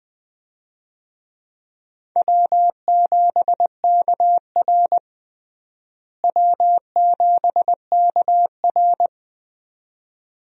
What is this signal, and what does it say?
Morse code, heard as a keyed tone, about 21 wpm, 700 Hz. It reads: W7KR W7KR
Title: USA Call Signs - 4 Characters - Repeated 2x - 20wpm